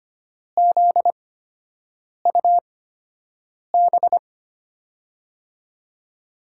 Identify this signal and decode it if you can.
Morse code, heard as a keyed tone, about 25 wpm, spaced as for 7 wpm, 700 Hz. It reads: ZUB
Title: USA Call Signs - 5 Characters - 8x Character Spacing - 25wpm